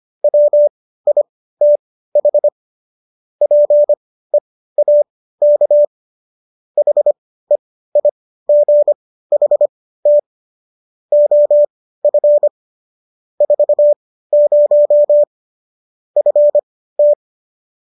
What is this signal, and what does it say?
Morse code, heard as a keyed tone, about 25 wpm, spaced as for 15 wpm, 590 Hz. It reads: WITH PEAK HEIGHT OF 40 FT